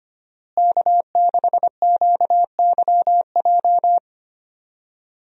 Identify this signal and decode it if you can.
Morse code, heard as a keyed tone, about 25 wpm, 700 Hz. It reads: K6QYJ
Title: USA Call Signs - 5 Characters - Repeated 3x - 25wpm